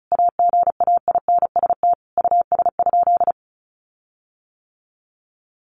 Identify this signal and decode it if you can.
Morse code, heard as a keyed tone, about 34 wpm, 700 Hz. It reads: AGAINST US?
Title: Sentences from Top 1500 Words 35wpm